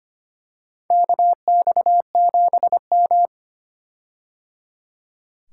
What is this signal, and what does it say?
Morse code, heard as a keyed tone, about 25 wpm, 700 Hz. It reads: KX7M
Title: K1USN SST QSO Practice - 25wpm